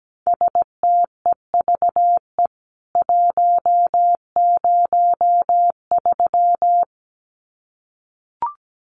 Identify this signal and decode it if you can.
Morse code, heard as a keyed tone, about 17 wpm, 700 Hz. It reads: STEVE 103
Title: CWT Exchange 17wpm